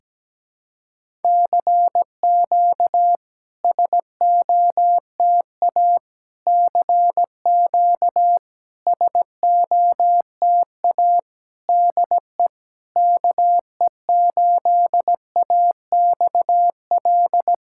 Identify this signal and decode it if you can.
Morse code, heard as a keyed tone, about 17 wpm, 690 Hz. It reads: CQ SOTA CQ SOTA DE KE8AXL
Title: SOTA QSO Practice 17wpm